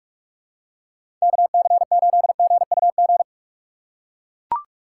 Morse code, heard as a keyed tone, about 43 wpm, 700 Hz. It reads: KC8GUG
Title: USA Call Signs - 6 Characters - 4x Character Spacing - 45wpm